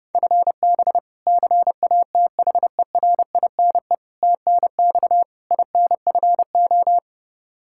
Morse code, heard as a keyed tone, about 32 wpm, 700 Hz. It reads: FB CATHERINE TNX INFO
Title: K1USN SST QSO Practice 30wpm